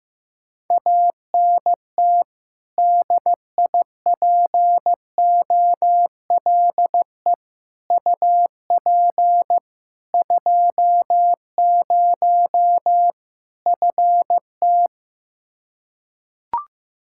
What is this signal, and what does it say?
Morse code, heard as a keyed tone, about 16 wpm, 700 Hz. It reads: ANT DIPOLE UP 20 FT
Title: Practice QSO - Exchange 2 - Info - 15wpm